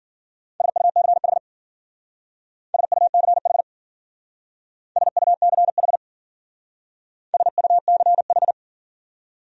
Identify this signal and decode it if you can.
Morse code, heard as a keyed tone, about 50 wpm, 700 Hz. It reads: SUCH SUCH SUCH SUCH
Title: Speed-Racing Top 200 Words 40wpm